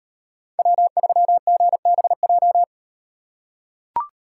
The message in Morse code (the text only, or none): W3GBJ